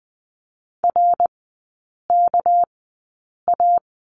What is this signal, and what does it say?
Morse code, heard as a keyed tone, about 20 wpm, spaced as for 8 wpm, 690 Hz. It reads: RKA
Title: 198 Lesson 32e - Letter k - Three Characters 20wpm